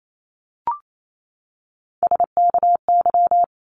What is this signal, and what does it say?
Morse code, heard as a keyed tone, about 28 wpm, 700 Hz. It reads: SKY
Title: Top 1001-1250 Words 28wpm